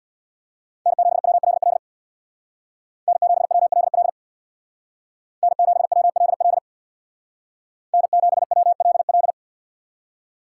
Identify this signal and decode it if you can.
Morse code, heard as a keyed tone, about 63 wpm, 700 Hz. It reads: N7WLL N7WLL N7WLL N7WLL